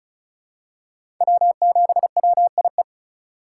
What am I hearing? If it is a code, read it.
Morse code, heard as a keyed tone, about 36 wpm, 690 Hz. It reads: W7WIE
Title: USA Call Signs - 5 Characters - 3x Character Spacing - 35wpm